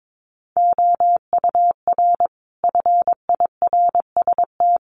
Morse code, heard as a keyed tone, about 22 wpm, 700 Hz. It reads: OUR FIRST
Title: Sentences from Top 1300 Words 22wpm